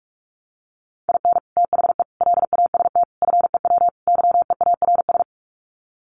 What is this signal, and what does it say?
Morse code, heard as a keyed tone, about 44 wpm, 700 Hz. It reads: IN THE LAST FEW YEARS
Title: Sets of 5 Words 45wpm